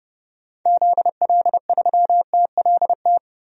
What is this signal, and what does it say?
Morse code, heard as a keyed tone, about 30 wpm, 700 Hz. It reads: ZL3TLT
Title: DXCC-100 Call Signs - 6 Characters - 30wpm